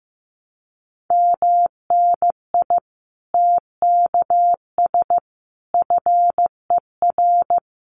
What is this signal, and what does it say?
Morse code, heard as a keyed tone, about 15 wpm, 700 Hz. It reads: MNI TKS FER
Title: Common QSO Phrases 15wpm